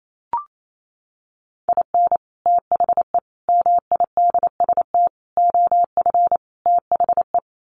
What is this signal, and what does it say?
Morse code, heard as a keyed tone, about 28 wpm, 700 Hz. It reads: IN THE MIDST OF THE
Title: Sets of 5 Words 28wpm